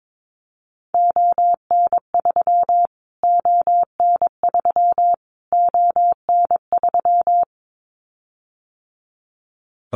Morse code, heard as a keyed tone, about 22 wpm, 700 Hz. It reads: ON3 ON3 ON3
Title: Callsign Prefixes for Top 25 DXCC - Repeated 3x - 22wpm